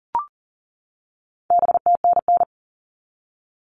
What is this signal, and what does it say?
Morse code, heard as a keyed tone, about 40 wpm, 700 Hz. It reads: BTNN
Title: All Cut Numbers - 4-Digit 40wpm